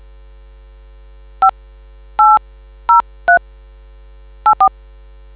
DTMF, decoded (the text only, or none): 580384